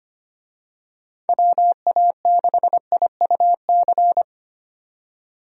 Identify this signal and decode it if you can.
Morse code, heard as a keyed tone, about 25 wpm, 700 Hz. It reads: WA6IUC
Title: USA Call Signs - 6 Characters - 25wpm